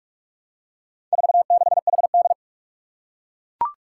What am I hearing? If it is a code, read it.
Morse code, heard as a keyed tone, about 44 wpm, 700 Hz. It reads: 46HD
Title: Four Challenging Letter-Number Combinations 45wpm